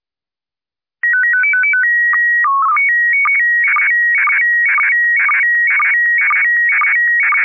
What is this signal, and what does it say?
SSTV (slow-scan television), PD 120.
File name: QSL Museo Prefectura PD120